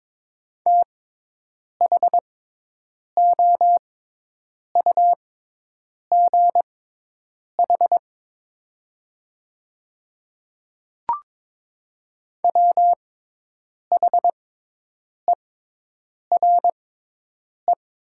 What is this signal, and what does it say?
Morse code, heard as a keyed tone, about 22 wpm, spaced as for 8 wpm, 700 Hz. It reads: THOUGH WHERE
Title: 6x Character Spacing - Top 500 Words - 22wpm